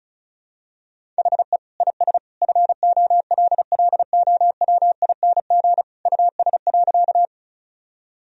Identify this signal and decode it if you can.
Morse code, heard as a keyed tone, about 36 wpm, 700 Hz. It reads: HE IS FOLLOWING US.
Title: Sentences from Top 1000 Words - Review Part 4 - 35wpm